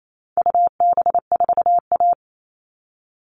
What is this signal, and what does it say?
Morse code, heard as a keyed tone, about 28 wpm, 700 Hz. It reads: UB4A
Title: All Cut Numbers - 4-Digit 28wpm